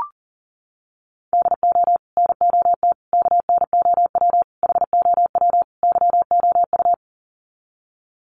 Morse code, heard as a keyed tone, about 40 wpm, 700 Hz. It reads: DO NOT KNOW HOW YOU
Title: Sets of 5 Words Encore 40wpm